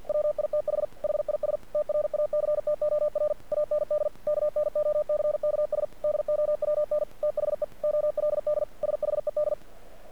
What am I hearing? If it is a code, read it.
Morse code, heard as a keyed tone, about 46 wpm, 600 Hz. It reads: WITH HIS TRACTOR AND KNOCKS DOWN THE OLD SHED
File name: archived / CRSnet